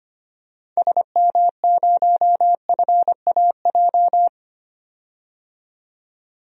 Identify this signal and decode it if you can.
Morse code, heard as a keyed tone, about 25 wpm, 700 Hz. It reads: SM0FAJ
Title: DXCC-100 Call Signs - 6 Characters - 25wpm